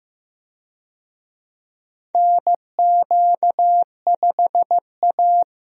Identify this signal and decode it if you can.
Morse code, heard as a keyed tone, about 16 wpm, 700 Hz. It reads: NQ5A
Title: USA Call Signs - 4 Characters - Repeated 2x - 15wpm